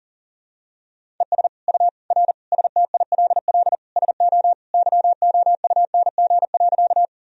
Morse code, heard as a keyed tone, about 40 wpm, 700 Hz. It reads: ES U R STILL SO YOUNG.
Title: Sentences from Top 400 Words and Abbreviations 40wpm